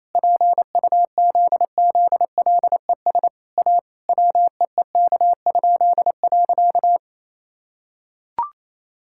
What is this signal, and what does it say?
Morse code, heard as a keyed tone, about 28 wpm, 700 Hz. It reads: PUZZLES A WEEK?.
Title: Pangram Challenge v2 - 28wpm